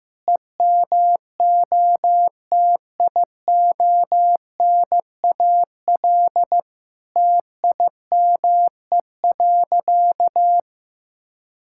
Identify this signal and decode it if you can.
Morse code, heard as a keyed tone, about 15 wpm, 700 Hz. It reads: EMOTIONAL TIME.
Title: Sentences from Top 1000 Words - Review Part 5 - 15wpm